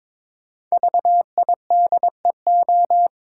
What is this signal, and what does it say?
Morse code, heard as a keyed tone, about 22 wpm, 700 Hz. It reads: VIDEO
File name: 5 Letter Words - First 3 Letters Given - 22wpm